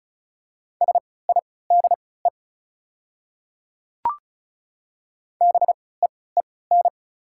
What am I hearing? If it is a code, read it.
Morse code, heard as a keyed tone, about 33 wpm, spaced as for 20 wpm, 690 Hz. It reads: SIDE BEEN
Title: 3x Character Spacing - Top 500 Words - 35wpm